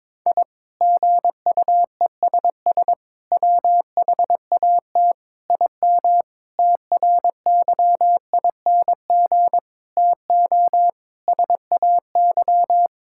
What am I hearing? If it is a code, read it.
Morse code, heard as a keyed tone, about 22 wpm, 700 Hz. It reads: I GUESS WHAT IM TRYING TO SAY
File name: practice / 22wpm / Sets of 8 Words - 22wpm